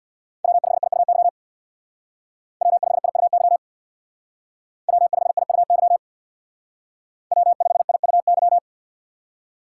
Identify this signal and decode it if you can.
Morse code, heard as a keyed tone, about 63 wpm, 700 Hz. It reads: W5IUY W5IUY W5IUY W5IUY